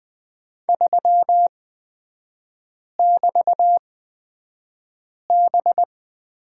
Morse code, heard as a keyed tone, about 20 wpm, spaced as for 5 wpm, 700 Hz. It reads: 3<BT>B
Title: 224 Lesson 36e - BT - Three Characters 20wpm